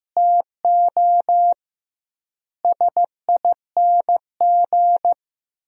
Morse code, heard as a keyed tone, about 15 wpm, 700 Hz. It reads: TO SING
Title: 2x Word-Spacing - Sentences from Top 400 Words - 15wpm